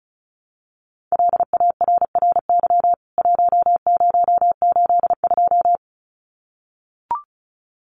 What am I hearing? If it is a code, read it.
Morse code, heard as a keyed tone, about 34 wpm, 700 Hz. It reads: LARRY 1082
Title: CWT Exchange 35wpm